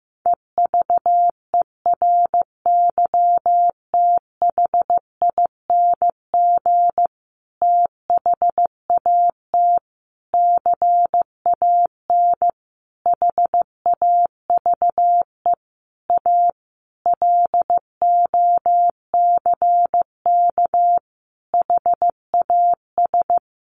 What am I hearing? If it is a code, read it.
Morse code, heard as a keyed tone, about 15 wpm, 700 Hz. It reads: EVERYTHING THAT CAN HAVE A LOCK HAS
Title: Sentences from Top 1300 Words 15wpm